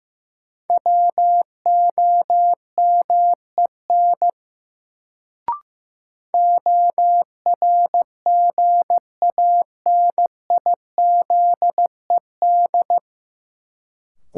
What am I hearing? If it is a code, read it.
Morse code, heard as a keyed tone, about 15 wpm, 700 Hz. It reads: WOMEN ORGANIZED
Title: Sentences from Top 1300 Words - New Word Drill Alt - 15wpm